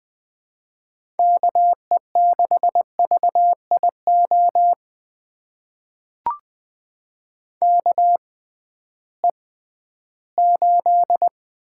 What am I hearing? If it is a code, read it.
Morse code, heard as a keyed tone, about 20 wpm, spaced as for 13 wpm, 700 Hz. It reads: KE6VIO KE8
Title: USA Call Signs - 6 Characters - 6x Character Spacing - 20wpm